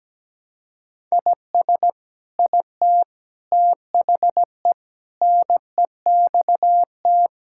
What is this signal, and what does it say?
Morse code, heard as a keyed tone, about 17 wpm, 700 Hz. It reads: IS IT THE NEXT
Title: Sentences from Top 1000 Words - Review Part 6 - 17wpm